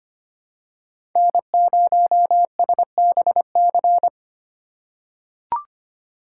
Morse code, heard as a keyed tone, about 25 wpm, 700 Hz. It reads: N0SBC
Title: USA Call Signs - 5 Characters - 4x Character Spacing - 25wpm